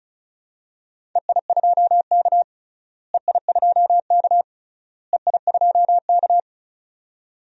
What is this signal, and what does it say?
Morse code, heard as a keyed tone, about 36 wpm, 700 Hz. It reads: EI2K EI2K EI2K
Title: DXCC-100 Call Signs - 4 Characters - Repeated 3x - 35wpm